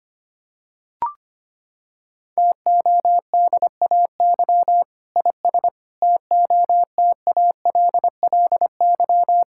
Morse code, heard as a keyed tone, about 25 wpm, 700 Hz. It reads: TODAY IS TOTALLY
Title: Sentences from Top 1600 Words 25wpm